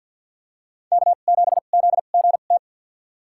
Morse code, heard as a keyed tone, about 46 wpm, spaced as for 35 wpm, 700 Hz. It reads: K7ZGT
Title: USA Call Signs - 2x Character Spacing - 50wpm